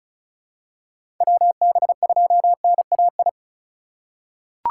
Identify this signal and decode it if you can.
Morse code, heard as a keyed tone, about 36 wpm, 700 Hz. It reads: WB2NAI
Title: USA Call Signs - 6 Characters - 2x Character Spacing - 35wpm